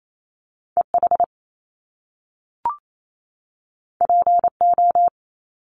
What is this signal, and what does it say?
Morse code, heard as a keyed tone, about 28 wpm, 700 Hz. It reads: EH PO
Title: Top Two Letter Combinations in Words 28wpm